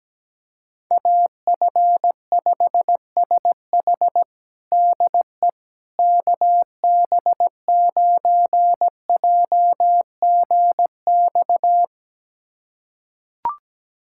Morse code, heard as a keyed tone, about 17 wpm, 700 Hz. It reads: AF5SH DE KB9JGX
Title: USA Callsign DE Callsign 17wpm